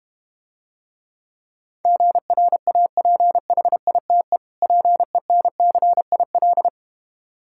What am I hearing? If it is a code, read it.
Morse code, heard as a keyed tone, about 32 wpm, 700 Hz. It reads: GRAPHITE PENCIL